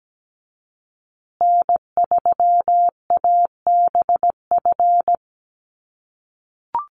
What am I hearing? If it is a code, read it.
Morse code, heard as a keyed tone, about 17 wpm, 700 Hz. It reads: N3ABF
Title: USA Call Signs Repeated 2x 17wpm